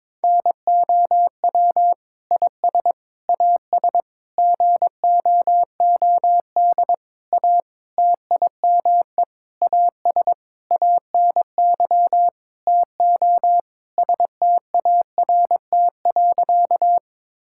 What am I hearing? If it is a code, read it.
Morse code, heard as a keyed tone, about 22 wpm, 700 Hz. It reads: NOW IS AS GOOD A TIME AS ANY TO START.
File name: practice / 22wpm / Sentences with 8-Word Phrases - 22wpm